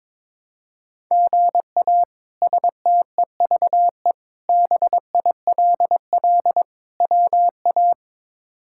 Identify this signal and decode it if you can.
Morse code, heard as a keyed tone, about 22 wpm, 700 Hz. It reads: GA STEVE BILL WA
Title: K1USN SST QSO Practice - 22wpm